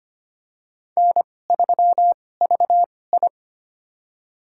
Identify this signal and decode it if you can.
Morse code, heard as a keyed tone, about 25 wpm, spaced as for 18 wpm, 700 Hz. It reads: N3VI
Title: USA Call Signs - 4 Characters - 2x Character Spacing - 25wpm